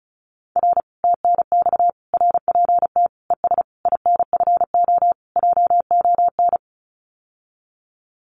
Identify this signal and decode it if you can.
Morse code, heard as a keyed tone, about 36 wpm, 700 Hz. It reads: R TNX RPT ES INFO JON
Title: Common QSO Phrases Encore 35wpm